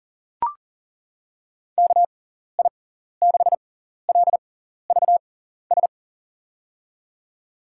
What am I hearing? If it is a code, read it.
Morse code, heard as a keyed tone, about 40 wpm, spaced as for 14 wpm, 700 Hz. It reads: KI6LVS
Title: USA Call Signs - 6 Characters - 6x Character Spacing - 40wpm